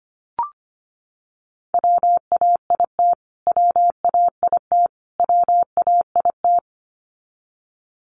Morse code, heard as a keyed tone, about 25 wpm, 700 Hz. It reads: WAIT WAIT WAIT
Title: Top 500 Words Repeated 25wpm